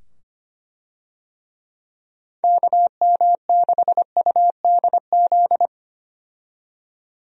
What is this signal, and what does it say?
Morse code, heard as a keyed tone, about 25 wpm, 700 Hz. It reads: KM6UDZ
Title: USA Call Signs - 6 Characters - 25wpm